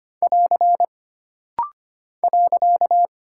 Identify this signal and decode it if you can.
Morse code, heard as a keyed tone, about 25 wpm, 700 Hz. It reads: <AR> .
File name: Single Punctuation-Prosign - Rapid-Fire - 25wpm